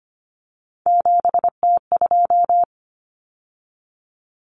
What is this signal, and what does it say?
Morse code, heard as a keyed tone, about 25 wpm, 700 Hz. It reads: 7T2
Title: Common Cut Numbers - 3-Digit 25wpm